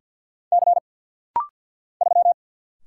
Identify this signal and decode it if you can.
Morse code, heard as a keyed tone, about 50 wpm, 700 Hz. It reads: C 3
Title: Single Letter-Number Alt - Rapid-Fire 50wpm